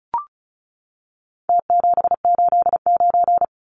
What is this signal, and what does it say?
Morse code, heard as a keyed tone, about 34 wpm, 700 Hz. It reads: T789
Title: Numbers - Contest Style - 4-Digit 35wpm